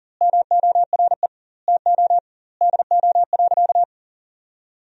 Morse code, heard as a keyed tone, about 40 wpm, 700 Hz. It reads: MORE TO DO.
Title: 2x Word-Spacing - Sentences from Top 100 Words - 40wpm